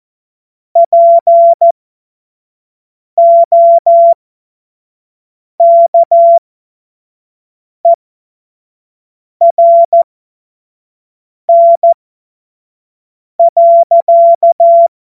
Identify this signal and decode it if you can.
Morse code, heard as a keyed tone, about 14 wpm, spaced as for 5 wpm, 680 Hz. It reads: POKERN.
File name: Morsekurs / Witze / Witz 08. Februar 2022 25 F70